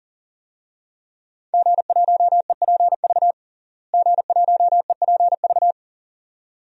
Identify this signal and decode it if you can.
Morse code, heard as a keyed tone, about 40 wpm, 700 Hz. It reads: G1EPV G1EPV
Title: DXCC-100 Call Signs - 5 Characters - Repeated 2x - 40wpm